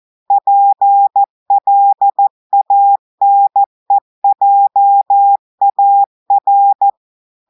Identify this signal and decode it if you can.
Morse code, heard as a keyed tone, about 14 wpm, 800 Hz. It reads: PLANEJAR
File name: audios / portugais / portugais texte en clair 2 vitesse 14 mots minutes